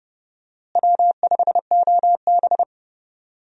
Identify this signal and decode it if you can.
Morse code, heard as a keyed tone, about 29 wpm, 700 Hz. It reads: W5OB
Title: USA Call Signs - 4 Characters - Repeated 2x - 30wpm